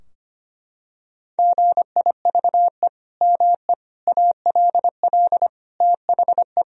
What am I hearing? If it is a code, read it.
Morse code, heard as a keyed tone, about 25 wpm, 700 Hz. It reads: GIVE ME ALL THE
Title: Sentences from Top 1000 Words - Review Part 3 - 25wpm